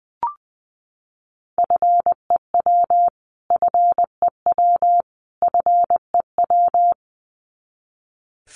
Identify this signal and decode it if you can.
Morse code, heard as a keyed tone, about 20 wpm, 700 Hz. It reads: FEW FEW FEW
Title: Top 300 Words Repeated 20wpm